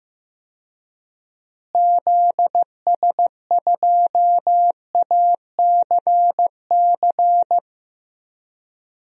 Morse code, heard as a keyed tone, about 15 wpm, 700 Hz. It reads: ZS2ACC
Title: DXCC-100 Call Signs - 6 Characters - Repeated 3x - 15wpm